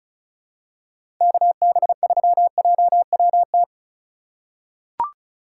Morse code, heard as a keyed tone, about 36 wpm, 700 Hz. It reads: KB3JWT
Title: USA Call Signs - 4x Character Spacing - 35wpm